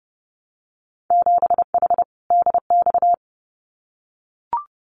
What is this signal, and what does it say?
Morse code, heard as a keyed tone, about 30 wpm, 700 Hz. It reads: 7H DX
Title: Winter Field Day Exchanges Encore 30wpm